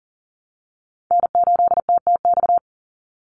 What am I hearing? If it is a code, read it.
Morse code, heard as a keyed tone, about 40 wpm, 700 Hz. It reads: N8TTX
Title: USA Call Signs 40wpm